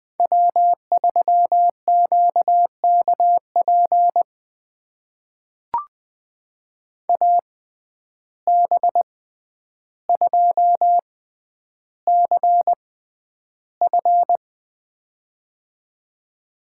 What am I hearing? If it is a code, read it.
Morse code, heard as a keyed tone, about 20 wpm, spaced as for 10 wpm, 700 Hz. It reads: W3QKP AB2CF
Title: USA Call Signs - 5 Characters - 6x Character Spacing - 20wpm